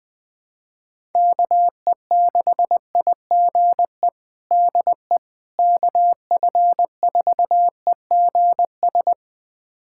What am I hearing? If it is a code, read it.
Morse code, heard as a keyed tone, about 20 wpm, 700 Hz. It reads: KE6IGE DE KF4EGS
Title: USA Callsign DE Callsign - 6 Characters - 20wpm